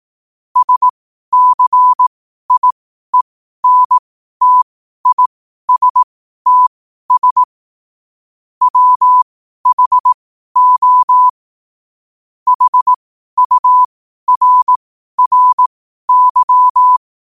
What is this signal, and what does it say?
Morse code, heard as a keyed tone, about 18 wpm, spaced as for 12 wpm, 1000 Hz. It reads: SCIENTISTS WHO HURRY
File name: qotdfiles / 12wpm